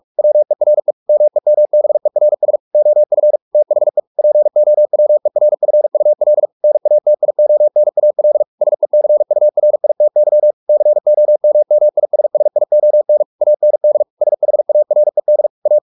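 Morse code, heard as a keyed tone, about 44 wpm, 600 Hz. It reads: WERE MEMBERS OF THE POWERFUL NATIONAL SECURITY COMMISSION AND SHARED A
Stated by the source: Morse practice at 45 wpm, the novel Never by Ken Follett in English, from CW med Gnister.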